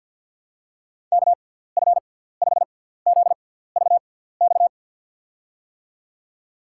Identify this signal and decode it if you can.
Morse code, heard as a keyed tone, about 50 wpm, spaced as for 17 wpm, 700 Hz. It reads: KF5ZVX
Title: USA Call Signs - 6 Characters - 6x Character Spacing - 50wpm